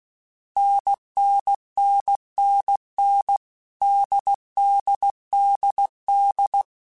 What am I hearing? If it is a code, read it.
Morse code, heard as a keyed tone, about 16 wpm, 800 Hz. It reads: NNNNN DDDD
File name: Difficult Combos / Difficult Combos-mp3s / NDB NDB6 @15